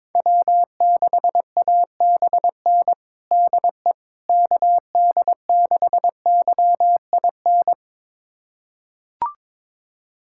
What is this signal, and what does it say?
Morse code, heard as a keyed tone, about 22 wpm, 700 Hz. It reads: W6ABN DE KD6YIN
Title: USA Callsign DE Callsign 22wpm